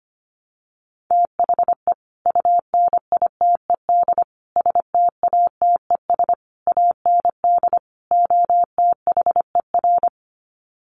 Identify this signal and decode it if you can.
Morse code, heard as a keyed tone, about 25 wpm, 700 Hz. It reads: THE UNITED STATES AND OTHER
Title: Sets of 5 Words Encore 25wpm